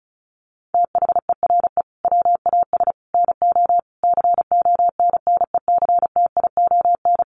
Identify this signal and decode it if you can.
Morse code, heard as a keyed tone, about 34 wpm, 700 Hz. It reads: THERE WAS NO CONNECTION
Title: Sentences from Top 1300 Words 35wpm